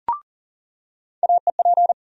Morse code, heard as a keyed tone, about 40 wpm, 700 Hz. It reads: AEP
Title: USA Call Sign Postfixes - 3 Letters 40wpm